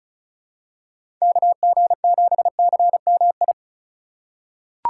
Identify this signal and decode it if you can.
Morse code, heard as a keyed tone, about 36 wpm, 700 Hz. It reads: KG7CMI
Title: USA Call Signs - 3x Character Spacing - 35wpm